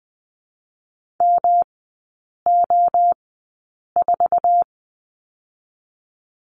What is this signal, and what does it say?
Morse code, heard as a keyed tone, about 20 wpm, 700 Hz. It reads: M O 4
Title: 114 Lesson 19e - Letter w - Three Characters 20wpm